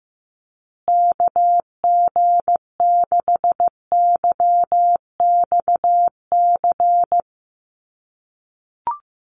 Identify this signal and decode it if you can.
Morse code, heard as a keyed tone, about 15 wpm, 700 Hz. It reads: KG6YXC
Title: USA Call Signs Repeated 2x 15wpm